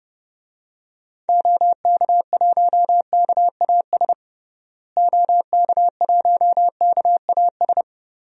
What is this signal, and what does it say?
Morse code, heard as a keyed tone, about 30 wpm, 700 Hz. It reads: OK1KAS OK1KAS
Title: DXCC-100 Call Signs - 6 Characters - Repeated 3x - 30wpm